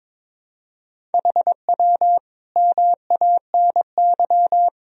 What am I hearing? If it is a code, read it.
Morse code, heard as a keyed tone, about 22 wpm, 700 Hz. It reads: HW MANY
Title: Sentences from Top 400 Words and Abbreviations 22wpm